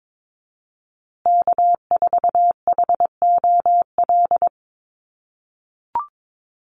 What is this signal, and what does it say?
Morse code, heard as a keyed tone, about 22 wpm, 700 Hz. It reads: K4HOL
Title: USA Call Signs Repeated 3x 22wpm